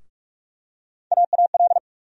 Morse code, heard as a keyed tone, about 46 wpm, 700 Hz. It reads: AAL
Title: USA Call Sign Postfixes - 3 Letters 45wpm